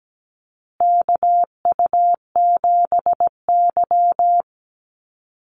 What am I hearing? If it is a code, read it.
Morse code, heard as a keyed tone, about 17 wpm, 700 Hz. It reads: KU7Y
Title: Call Signs Repeated 2x - 17wpm